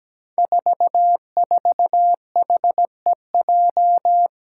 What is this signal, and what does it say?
Morse code, heard as a keyed tone, about 17 wpm, 700 Hz. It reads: 44HEJ
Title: Five Challenging Letter-Number Combinations 17wpm